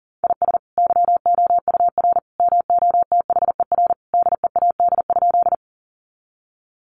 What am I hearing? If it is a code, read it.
Morse code, heard as a keyed tone, about 38 wpm, 700 Hz. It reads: IS YOUR MOTHER DEAD?
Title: Sentences from Top 1500 Words 40wpm